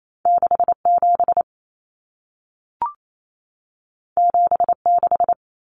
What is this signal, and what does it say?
Morse code, heard as a keyed tone, about 28 wpm, 700 Hz. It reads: 67 76
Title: Common Cut Numbers - 2-Digit 28wpm